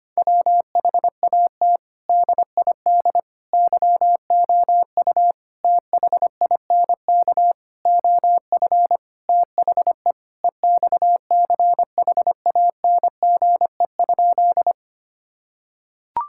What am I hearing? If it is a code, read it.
Morse code, heard as a keyed tone, about 25 wpm, 700 Hz. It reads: WHAT DID YOU THINK OF THE EXCHANGE?
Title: Sentences from Top 1000 Words - Review Part 6 - 25wpm